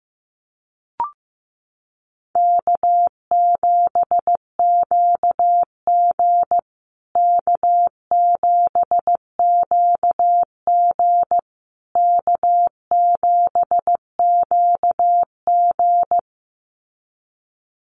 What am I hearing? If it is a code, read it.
Morse code, heard as a keyed tone, about 15 wpm, 700 Hz. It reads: K7QG K7QG K7QG
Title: USA Call Signs Repeated 3x 15wpm